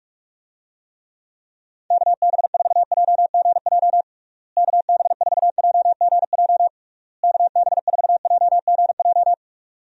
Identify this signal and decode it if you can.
Morse code, heard as a keyed tone, about 41 wpm, 700 Hz. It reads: KB4JGJ KB4JGJ KB4JGJ
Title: USA Call Signs - 6 Characters - Repeated 3x - 45wpm